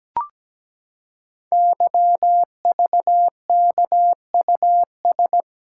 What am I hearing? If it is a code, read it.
Morse code, heard as a keyed tone, about 17 wpm, 700 Hz. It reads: YVKUS
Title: Code Groups - 5 Letters - 17wpm